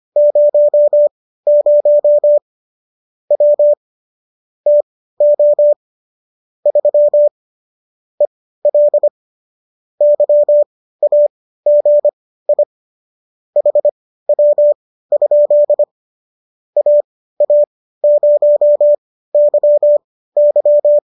Morse code, heard as a keyed tone, about 25 wpm, spaced as for 15 wpm, 590 Hz. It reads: 00 W TO 3 EL YAGI HW? AA0YY